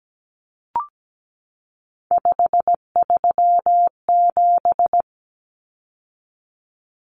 Morse code, heard as a keyed tone, about 17 wpm, 700 Hz. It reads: 537
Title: Numbers - 3-Digit 17wpm